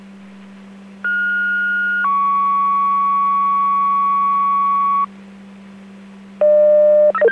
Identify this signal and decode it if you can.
Two-tone page, heard as a one-second tone, then a three-second tone, about 1430 then 1120 Hz. It is heard from 1.0 to 5.0 s.